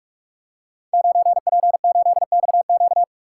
Morse code, heard as a keyed tone, about 44 wpm, 700 Hz. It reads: 9P8XQ
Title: Code Groups - 5 Letter:Numbers - 45wpm